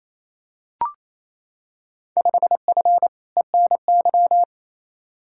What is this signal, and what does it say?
Morse code, heard as a keyed tone, about 28 wpm, 700 Hz. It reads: 5F ENY
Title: Field Day Exchanges Encore - 28wpm